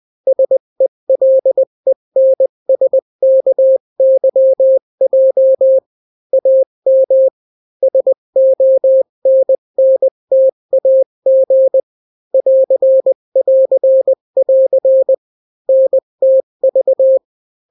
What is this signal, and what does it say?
Morse code, heard as a keyed tone, about 20 wpm, 530 Hz. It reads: SELENSKYJ AM SONNTAG <AR><AR><AR> NTV
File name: daily / 18wpm